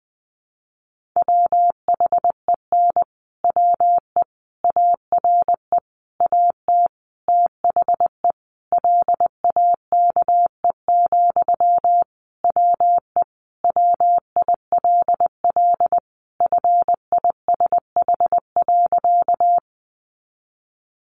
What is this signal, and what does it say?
Morse code, heard as a keyed tone, about 20 wpm, 700 Hz. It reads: WHEN WE ARE AT THE LAKE, WE WILL FISH.
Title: Sentences from Top 1100 Words 20wpm